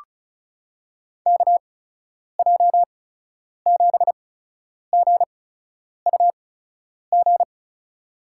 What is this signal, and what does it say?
Morse code, heard as a keyed tone, about 36 wpm, spaced as for 10 wpm, 700 Hz. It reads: KJ7GUG
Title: USA Call Signs - 6 Characters - 8x Character Spacing - 35wpm